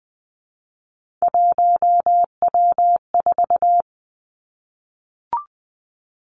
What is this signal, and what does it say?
Morse code, heard as a keyed tone, about 20 wpm, 700 Hz. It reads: 1W4
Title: Three Challenging Letter-Number Combinations 20wpm